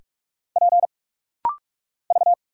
Morse code, heard as a keyed tone, about 46 wpm, 690 Hz. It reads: P V
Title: Single Letters - Rapid-Fire 45wpm